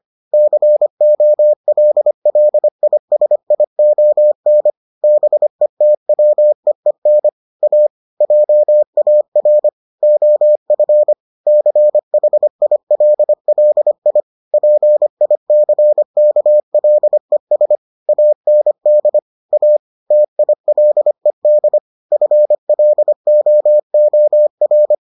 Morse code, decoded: COLLISION BETWEEN A JAR OF CHILLI PICKLES AND A TILED FLOOR

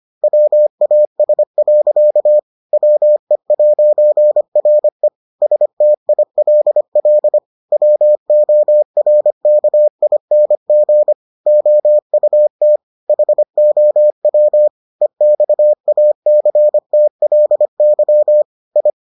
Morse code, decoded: WAS. WE'RE STILL WORKING OUT HOW EXACTLY I

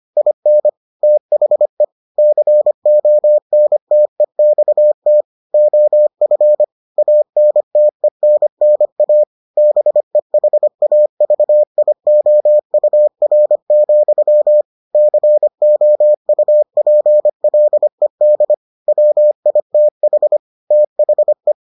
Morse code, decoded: IN THE CONTEXT OF ANTENNA BEHAVIOUR, COUPLED WITH THE

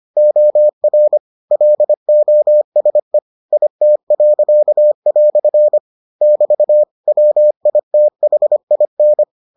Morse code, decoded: OR LOSE IT." <BT> WITHIN

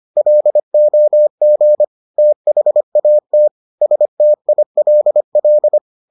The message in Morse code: LOG THAT STILL

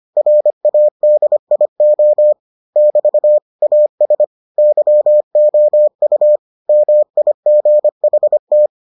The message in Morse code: RADIO <BT> AS YOU MIGHT

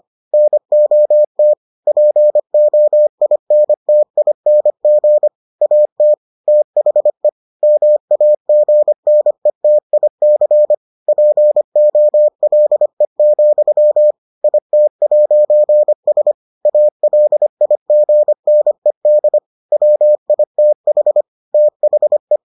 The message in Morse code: NOT POINTING AT THE MAGNETIC POLE, IT'S ALIGNED WITH THE

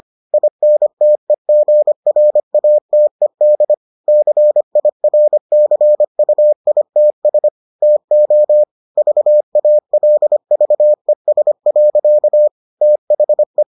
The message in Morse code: INTEGRATED CIRCUITS TO VALVES. THE

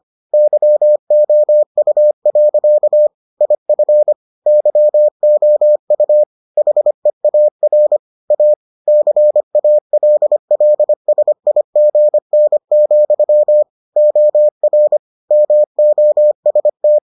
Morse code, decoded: YOU. IF YOU HEAR A CALLSIGN, OR MOST

